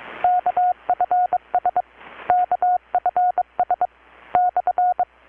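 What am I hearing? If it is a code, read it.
Morse code, heard as a keyed tone, about 22 wpm, 720 Hz. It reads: KFS KFS /
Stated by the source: Morse from KPH's Night of Nights XX maritime radiotelegraphy event, received in Chiapas, Mexico about 3,800 km away on a Kenwood R-600 with a grounded 10 m long-wire antenna.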